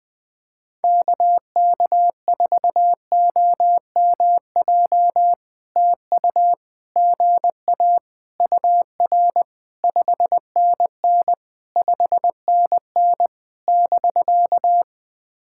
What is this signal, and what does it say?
Morse code, heard as a keyed tone, about 20 wpm, 700 Hz. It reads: KK4OMJ TU GA UR 5NN 5NN <BK>
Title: POTA QSO Practice - 20wpm